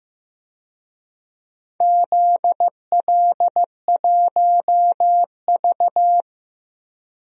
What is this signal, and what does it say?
Morse code, heard as a keyed tone, about 16 wpm, 700 Hz. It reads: ZL1V
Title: DXCC-100 Call Signs - 4 Characters - 15wpm